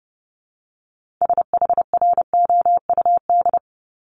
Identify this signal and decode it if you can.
Morse code, heard as a keyed tone, about 30 wpm, 700 Hz. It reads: SHROUD
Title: Multi-letter Phonemes Repeated 3x 30wpm